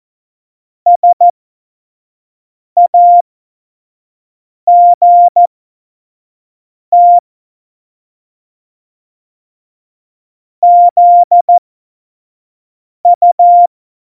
Morse code, heard as a keyed tone, about 14 wpm, spaced as for 5 wpm, 700 Hz. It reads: SAGT ZU
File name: Morsekurs / Witze / Witz 06. November 2018, 25 F70